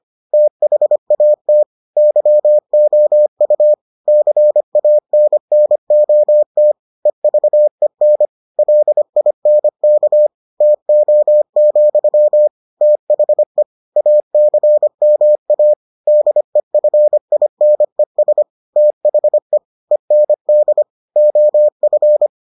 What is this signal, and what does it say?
Morse code, heard as a keyed tone, about 25 wpm, 600 Hz. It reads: THAT YOU CANNOT EVEN LINK TO, THE ACMA DEFINES THE END OF